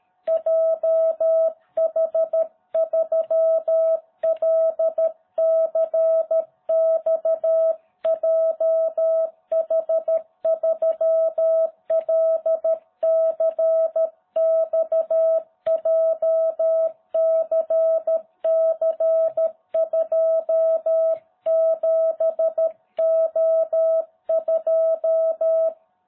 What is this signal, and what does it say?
Morse code, heard as a keyed tone, about 13 wpm, 660 Hz. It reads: JH3LCX JH3LCX JCC27O2